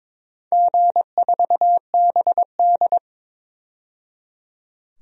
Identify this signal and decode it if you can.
Morse code, heard as a keyed tone, about 22 wpm, 700 Hz. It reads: G4BD
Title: DXCC-100 Call Signs - 4 Characters - 22wpm